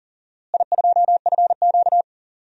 Morse code, heard as a keyed tone, about 40 wpm, 700 Hz. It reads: I2FQ
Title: DXCC-100 Call Signs - 4 Characters - Repeated 3x - 40wpm